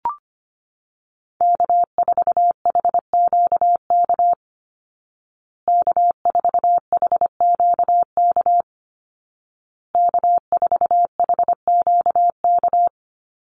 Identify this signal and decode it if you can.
Morse code, heard as a keyed tone, about 25 wpm, 700 Hz. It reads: K4HQK K4HQK K4HQK
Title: Call Signs Repeated 2x - 25wpm